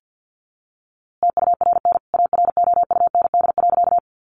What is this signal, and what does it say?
Morse code, heard as a keyed tone, about 50 wpm, 700 Hz. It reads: TURN AROUND.
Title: A Sound Of Thunder 50wpm